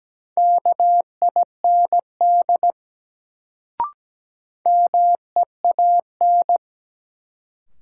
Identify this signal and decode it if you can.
Morse code, heard as a keyed tone, about 17 wpm, 700 Hz. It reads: KIND MEAN
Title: Sentences from Top 200 Words - New Word Drill - 17wpm